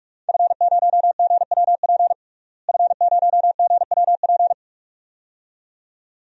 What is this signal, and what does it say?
Morse code, heard as a keyed tone, about 46 wpm, 700 Hz. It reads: F0GWP F0GWP
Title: DXCC-100 Call Signs - 5 Characters - Repeated 2x - 45wpm